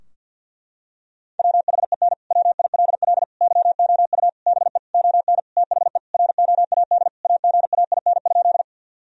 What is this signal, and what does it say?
Morse code, heard as a keyed tone, about 50 wpm, 700 Hz. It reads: WHEN WILL YOU BE ON THE ROAD AGAIN?
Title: Sentences from Top 1000 Words - Review Part 2 - 50wpm